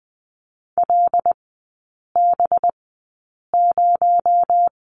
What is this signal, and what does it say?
Morse code, heard as a keyed tone, about 20 wpm, spaced as for 8 wpm, 700 Hz. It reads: LB0
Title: 218 Lesson 35e - Number 0 - Three Characters 20wpm